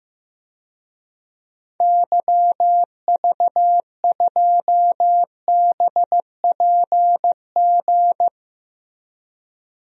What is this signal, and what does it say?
Morse code, heard as a keyed tone, about 15 wpm, 700 Hz. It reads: YV2BPG
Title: DXCC-100 Call Signs - 6 Characters - Repeated 2x - 15wpm